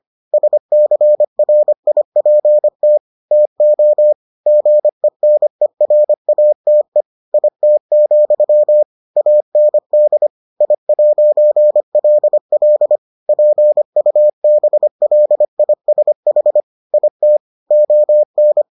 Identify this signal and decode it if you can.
Morse code, heard as a keyed tone, about 25 wpm, 600 Hz. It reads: SCRIPT TO GENERATE IT, AND I'LL PUBLISH IT ON